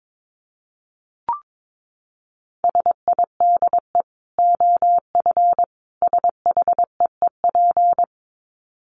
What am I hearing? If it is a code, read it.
Morse code, heard as a keyed tone, about 22 wpm, 700 Hz. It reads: SIDE OF SHEEP
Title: Dit Challenge 22wpm